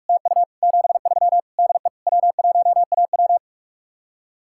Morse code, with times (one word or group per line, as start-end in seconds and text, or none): 0.1-0.4 TU
0.6-1.4 73
1.6-1.9 DE
2.1-3.4 W1AW